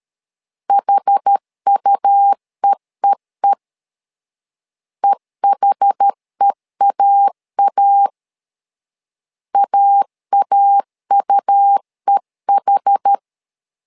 Morse code, heard as a keyed tone, about 13 wpm, 800 Hz. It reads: HUEEE EHEAA AAUEH
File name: LEZIONI / 2 UHDN / 2 UH 12wpm 1439